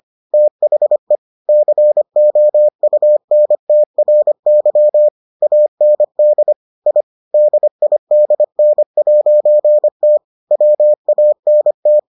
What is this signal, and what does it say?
Morse code, heard as a keyed tone, about 25 wpm, 600 Hz. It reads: THE COUNTRY AND I DIDN'T WANT